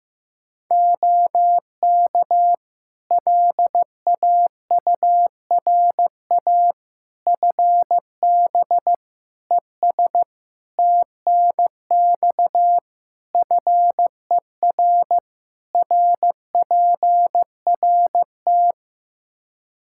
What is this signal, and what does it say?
Morse code, heard as a keyed tone, about 16 wpm, 700 Hz. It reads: OK LAURA FB ES TNX FER RPRT
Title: K1USN SST QSO Practice 15wpm